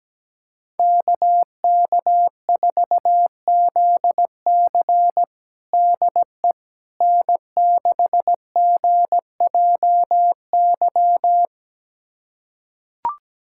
Morse code, heard as a keyed tone, about 17 wpm, 700 Hz. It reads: KK4ZC DE N6GJY
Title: USA Callsign DE Callsign - 5 Characters - 17wpm